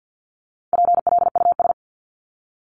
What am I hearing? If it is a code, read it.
Morse code, heard as a keyed tone, about 50 wpm, 700 Hz. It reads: PLUS
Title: Top 1001-1250 Words 50wpm